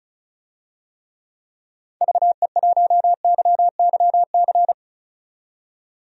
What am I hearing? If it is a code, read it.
Morse code, heard as a keyed tone, about 34 wpm, 700 Hz. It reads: VE1YYC
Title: DXCC-100 Call Signs - 6 Characters - 35wpm